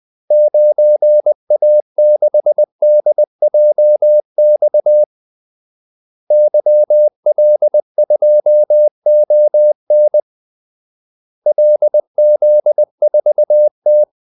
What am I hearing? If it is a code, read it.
Morse code, heard as a keyed tone, about 20 wpm, 600 Hz. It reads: 9A6DJX YL2ON LZ4T
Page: Kaldesignaler 20 wpm